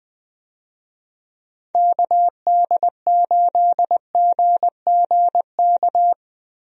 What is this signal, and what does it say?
Morse code, heard as a keyed tone, about 20 wpm, 700 Hz. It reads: KD8GGK
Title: USA Call Signs - 6 Characters - 20wpm